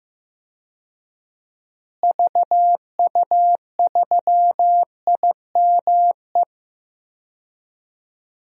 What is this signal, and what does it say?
Morse code, heard as a keyed tone, about 15 wpm, 700 Hz. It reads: VU3IME
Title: DXCC-100 Call Signs - 6 Characters - Repeated 3x - 15wpm